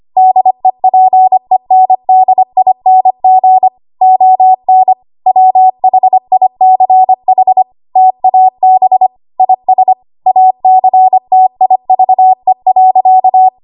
Morse code, heard as a keyed tone, about 25 wpm, 750 Hz. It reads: DEPENDING ON WHICH TAB IS ACTIVE.
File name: Archive / 25 WPM / 25WPM